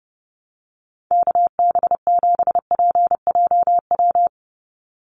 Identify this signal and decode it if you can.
Morse code, heard as a keyed tone, about 29 wpm, 700 Hz. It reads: KB7PJW
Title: USA Call Signs 30wpm